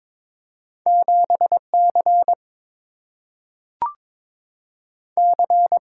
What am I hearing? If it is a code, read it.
Morse code, heard as a keyed tone, about 22 wpm, 700 Hz. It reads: 7 C C
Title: Single-Letter Single-Number 22wpm